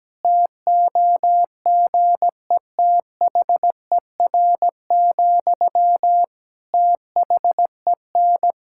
Morse code, heard as a keyed tone, about 17 wpm, 700 Hz. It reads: TOGETHER, THEN
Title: Sentences from Top 1600 Words 17wpm